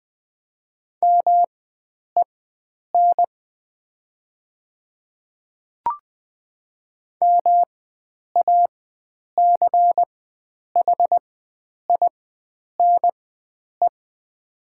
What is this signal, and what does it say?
Morse code, heard as a keyed tone, about 20 wpm, spaced as for 9 wpm, 700 Hz. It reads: MEN MACHINE
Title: 4x Character Spacing - Top 500 Words - 20wpm